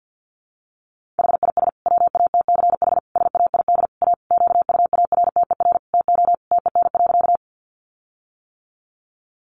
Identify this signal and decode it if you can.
Morse code, heard as a keyed tone, about 50 wpm, 700 Hz. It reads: HIS WATCH SAID A QUARTER TO TEN.
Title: Sentences from Top 1300 Words 50wpm